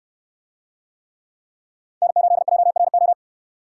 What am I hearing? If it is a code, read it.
Morse code, heard as a keyed tone, about 68 wpm, 700 Hz. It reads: N8JRO